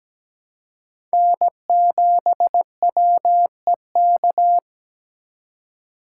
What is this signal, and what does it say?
Morse code, heard as a keyed tone, about 20 wpm, 700 Hz. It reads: N7WEK
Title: Speed Racing - USA 1x3 Call Signs 17wpm